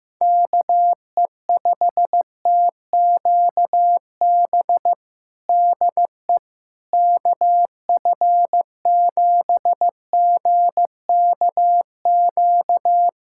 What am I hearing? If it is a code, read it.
Morse code, heard as a keyed tone, about 16 wpm, 700 Hz. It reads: KE5TQB DE KF7GKQ
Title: USA Callsign DE Callsign - 6 Characters - 15wpm